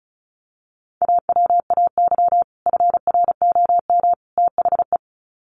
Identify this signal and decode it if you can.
Morse code, heard as a keyed tone, about 34 wpm, 700 Hz. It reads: AWAY FROM THE
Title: Sets of 3 Words 35wpm